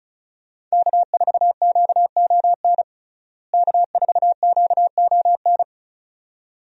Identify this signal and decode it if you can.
Morse code, heard as a keyed tone, about 36 wpm, 700 Hz. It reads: K4QON K4QON
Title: USA Call Signs - 5 Characters - Repeated 2x - 35wpm